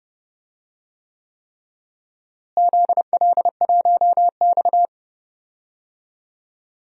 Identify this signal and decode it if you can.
Morse code, heard as a keyed tone, about 31 wpm, 700 Hz. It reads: ZL1X
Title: DXCC-100 Call Signs - 4 Characters - Repeated 3x - 30wpm